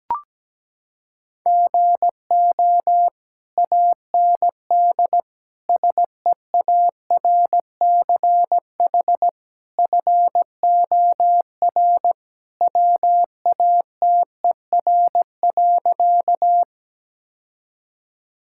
Morse code, heard as a keyed tone, about 17 wpm, 700 Hz. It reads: GO AND SEARCH FOR WATER.
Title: Sentences from Top 1000 Words - Review Part 5 - 17wpm